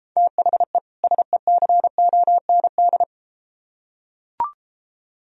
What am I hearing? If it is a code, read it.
Morse code, heard as a keyed tone, about 33 wpm, 700 Hz. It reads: THE SECOND